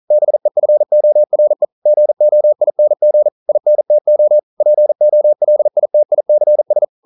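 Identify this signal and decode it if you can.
Morse code, heard as a keyed tone, about 42 wpm, 600 Hz. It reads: BEFORE GOING INTO POLITICS
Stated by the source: Morse code practice at 41 wpm, from CW med Gnister.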